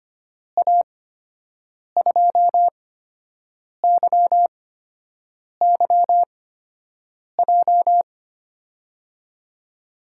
Morse code, heard as a keyed tone, about 25 wpm, spaced as for 7 wpm, 700 Hz. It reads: A2YYJ
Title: USA Call Signs - 6 Characters - 8x Character Spacing - 25wpm